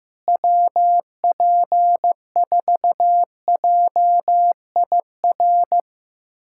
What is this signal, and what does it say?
Morse code, heard as a keyed tone, about 15 wpm, 700 Hz. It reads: WP4JIR
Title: USA Callsign DE Callsign 15wpm